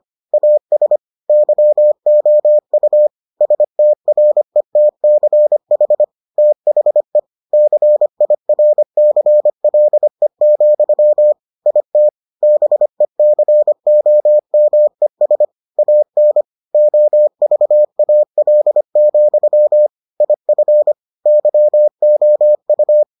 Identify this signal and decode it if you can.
Morse code, heard as a keyed tone, about 25 wpm, 600 Hz. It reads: AS YOU STRETCH THE CIRCLE, IT BECOMES AN OVAL, IF YOU